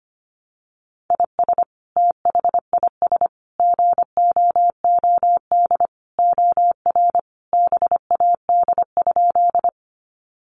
Morse code, heard as a keyed tone, about 25 wpm, 700 Hz. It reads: IS THIS GOOD OR BAD?
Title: Sentences from Top 700 Words 25wpm